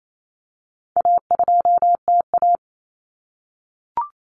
Morse code, heard as a keyed tone, about 28 wpm, 700 Hz. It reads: A2TA
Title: Common Cut Numbers - 4-Digit 28wpm